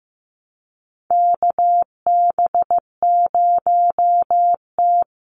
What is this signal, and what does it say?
Morse code, heard as a keyed tone, about 15 wpm, 700 Hz. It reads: KB0T
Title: USA Call Signs Repeated 3x 15wpm